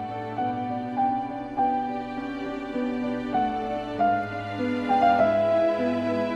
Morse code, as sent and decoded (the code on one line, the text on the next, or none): none